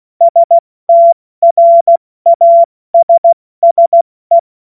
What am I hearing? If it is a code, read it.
Morse code, heard as a keyed tone, about 16 wpm, spaced as for 14 wpm, 680 Hz. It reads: STRASSE
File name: Morsekurs / Witze / Witz 31. Mai 2022 70 F80